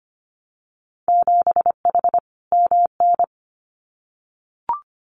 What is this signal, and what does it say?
Morse code, heard as a keyed tone, about 25 wpm, 700 Hz. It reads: 7H MN
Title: Winter Field Day Exchanges Encore 25wpm